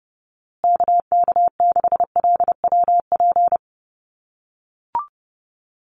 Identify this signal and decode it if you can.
Morse code, heard as a keyed tone, about 30 wpm, 700 Hz. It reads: KK6LWP
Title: USA Call Signs Repeated 2x 30wpm